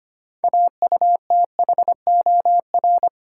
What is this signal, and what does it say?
Morse code, heard as a keyed tone, about 25 wpm, 690 Hz. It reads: AUTHOR
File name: practice / 25wpm / Easy Words with Hints - 25wpm